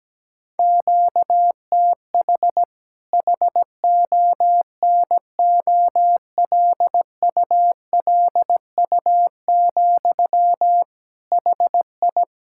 Morse code, heard as a keyed tone, about 17 wpm, 700 Hz. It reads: QTH HONOLULU, HI
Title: Practice QSO - Exchange 1 - Beginning - 17wpm